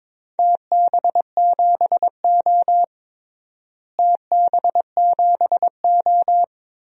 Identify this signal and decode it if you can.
Morse code, heard as a keyed tone, about 21 wpm, 700 Hz. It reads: TB7O TB7O
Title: DXCC-100 Call Signs - 4 Characters - Repeated 2x - 22wpm